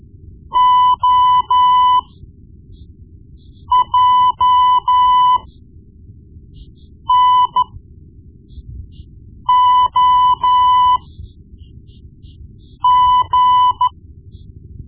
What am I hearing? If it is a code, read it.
Morse code, heard as a keyed tone, about 14 wpm, spaced as for 4 wpm, 1000 Hz. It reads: OJNOG
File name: Morse Lessons / Morse Lessons / Tape-1B